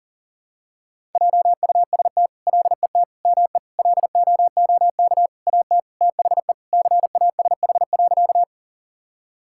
Morse code, heard as a keyed tone, about 40 wpm, 700 Hz. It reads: JUST LET ME LOOK AT THE CASH.
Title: Sentences from Top 1000 Words - Review Part 5 - 40wpm